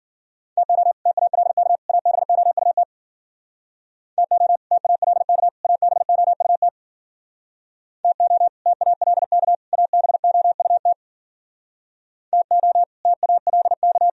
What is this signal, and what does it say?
Morse code, heard as a keyed tone, about 51 wpm, 700 Hz. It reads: TO TALK ABOUT TO TALK ABOUT TO TALK ABOUT TO TALK